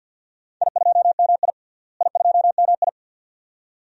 Morse code, heard as a keyed tone, about 46 wpm, 700 Hz. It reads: I2MI I2MI
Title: DXCC-100 Call Signs - 4 Characters - Repeated 2x - 50wpm